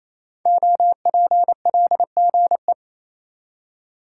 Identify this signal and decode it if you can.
Morse code, heard as a keyed tone, about 28 wpm, 690 Hz. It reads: OPLGE
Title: Code Groups - 5 Letters - 28wpm